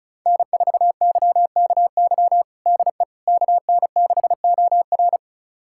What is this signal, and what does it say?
Morse code, heard as a keyed tone, about 34 wpm, 700 Hz. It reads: N4YKY DE KN6OR
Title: USA Callsign DE Callsign 35wpm